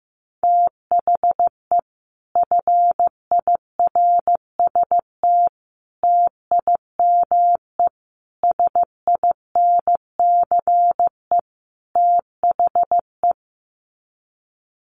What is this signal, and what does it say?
Morse code, heard as a keyed tone, about 15 wpm, 700 Hz. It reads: THE FIRST TIME SINCE THE
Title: Sets of 5 Words Encore 15wpm